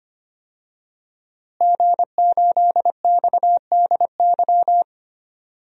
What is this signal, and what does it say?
Morse code, heard as a keyed tone, about 25 wpm, 700 Hz. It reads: G8XDY
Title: DXCC-100 Call Signs - 5 Characters - Repeated 3x - 25wpm